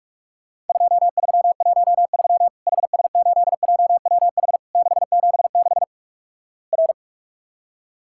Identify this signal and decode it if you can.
Morse code, heard as a keyed tone, about 48 wpm, 700 Hz. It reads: 2313 HS8JWH 676 R
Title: Worked All Europe DX Contest - QTC Exchange - 45wpm